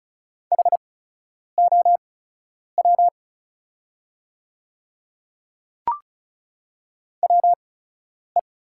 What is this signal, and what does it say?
Morse code, heard as a keyed tone, about 36 wpm, spaced as for 10 wpm, 690 Hz. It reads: HOW WE
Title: 8x Character Spacing - Top 200 Words - 35wpm